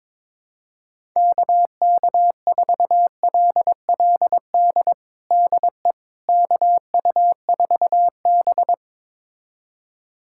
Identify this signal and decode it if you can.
Morse code, heard as a keyed tone, about 22 wpm, 700 Hz. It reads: KK4LLD DE KU4B
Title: USA Callsign DE Callsign 22wpm